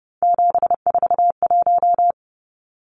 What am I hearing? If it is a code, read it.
Morse code, heard as a keyed tone, about 30 wpm, 700 Hz. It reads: 741
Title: Numbers - 3-Digit 30wpm